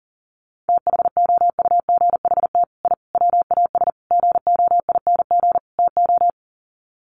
Morse code, heard as a keyed tone, about 40 wpm, 700 Hz. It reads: THOUGHT I WAS GOING TO
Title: Sets of 5 Words Encore2 40wpm